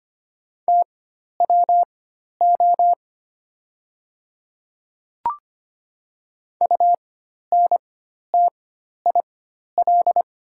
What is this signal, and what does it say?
Morse code, heard as a keyed tone, about 25 wpm, spaced as for 12 wpm, 700 Hz. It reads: TWO UNTIL
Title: 4x Character Spacing - Top 300 Words - 25wpm